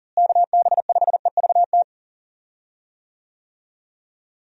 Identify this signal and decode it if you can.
Morse code, heard as a keyed tone, about 37 wpm, 700 Hz. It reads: KB5EVT
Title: USA Call Signs - 6 Characters - Repeated 3x - 40wpm